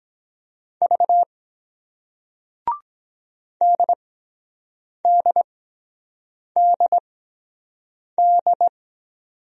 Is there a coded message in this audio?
Yes: Morse code from 0.8 to 8.7 s, heard as a keyed tone, about 23 wpm, spaced as for 7 wpm, 700 Hz. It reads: V DDDD